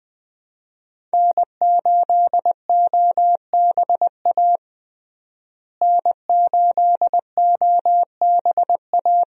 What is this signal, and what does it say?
Morse code, heard as a keyed tone, about 20 wpm, 700 Hz. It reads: N8OBA N8OBA
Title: USA Call Signs - 5 Characters - Repeated 3x - 20wpm